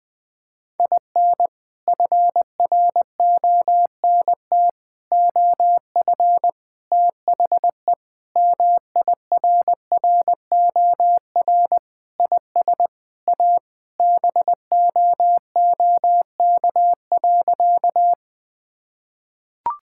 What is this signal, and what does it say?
Morse code, heard as a keyed tone, about 20 wpm, 700 Hz. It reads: IN FRONT OF THE MIRROR IS A BOOK.
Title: Sentences with 4-Word Phrases - 20wpm